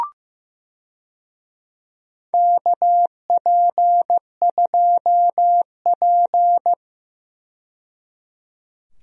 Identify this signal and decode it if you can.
Morse code, heard as a keyed tone, about 15 wpm, 700 Hz. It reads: KP2P
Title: DXCC-100 Call Signs - 4 Characters - 15wpm